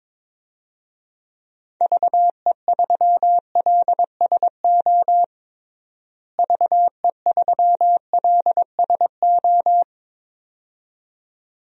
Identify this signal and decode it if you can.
Morse code, heard as a keyed tone, about 22 wpm, 700 Hz. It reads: VE3LSO VE3LSO
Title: DXCC-100 Call Signs - 6 Characters - Repeated 2x - 22wpm